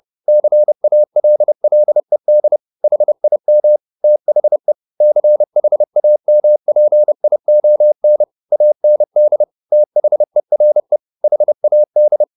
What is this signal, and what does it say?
Morse code, heard as a keyed tone, about 30 wpm, 600 Hz. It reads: CALLED HIM THE CHAMPION AND THERE HAD